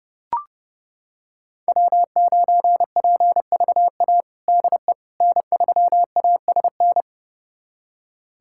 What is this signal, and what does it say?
Morse code, heard as a keyed tone, about 30 wpm, 700 Hz. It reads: W9PVA DE N3ASN
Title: USA Callsign DE Callsign - 5 Characters - 30wpm